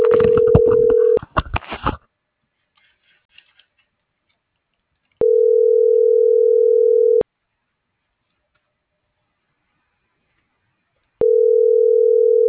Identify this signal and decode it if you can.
Ringback tone (North America), 440 + 480 Hz.